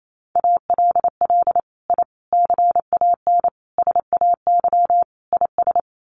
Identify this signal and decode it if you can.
Morse code, heard as a keyed tone, about 28 wpm, 700 Hz. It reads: ALL I CAN SAY IS
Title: Sets of 5 Words Encore2 28wpm